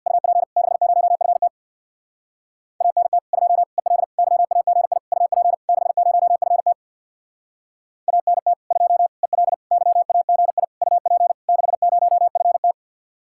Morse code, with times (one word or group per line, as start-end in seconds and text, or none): 0.1-0.4 UP
0.6-1.5 60FT
2.8-3.2 ANT
3.3-3.6 2
3.8-4.0 EL
4.2-5.0 YAGI
5.1-5.5 UP
5.7-6.7 60FT
8.1-8.5 ANT
8.7-9.1 2
9.2-9.5 EL
9.7-10.6 YAGI
10.8-11.3 UP
11.5-12.7 60FT